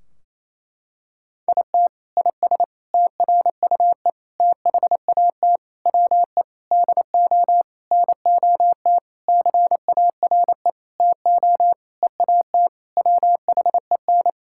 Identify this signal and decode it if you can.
Morse code, heard as a keyed tone, about 27 wpm, 700 Hz. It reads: IT IS TRUE THAT WE DO NOT CARE TO EAT WHEN
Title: Sentences from Top 1600 Words 28wpm